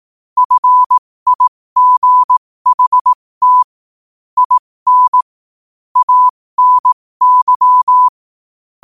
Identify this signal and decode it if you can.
Morse code, heard as a keyed tone, about 18 wpm, spaced as for 15 wpm, 1000 Hz. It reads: FIGHT IN ANY
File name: qotdfiles / 15wpm